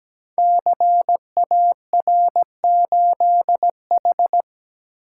Morse code, heard as a keyed tone, about 17 wpm, 700 Hz. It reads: CAR8H
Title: Code Groups - 5 Letter:Numbers - 17wpm